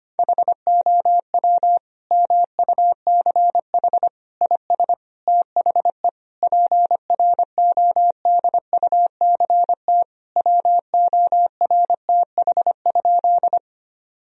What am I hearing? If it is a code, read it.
Morse code, heard as a keyed tone, about 25 wpm, 700 Hz. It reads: HOW MUCH IS THE PRODUCT WORTH?
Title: Sentences from Top 1000 Words - Review Part 2 - 25wpm